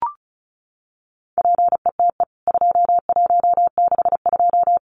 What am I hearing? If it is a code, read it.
Morse code, heard as a keyed tone, about 34 wpm, 700 Hz. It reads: PETE 2162
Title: CWT Exchange 35wpm